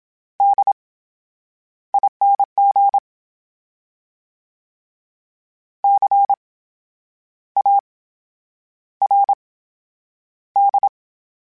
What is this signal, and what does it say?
Morse code, heard as a keyed tone, about 27 wpm, spaced as for 9 wpm, 800 Hz. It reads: DING CARD
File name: Special INGs / Special INGs-mp3s / VerySlowToFaster Ing @25